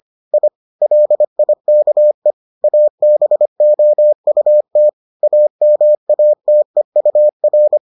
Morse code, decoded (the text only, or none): I LIKE ABOUT AMATEUR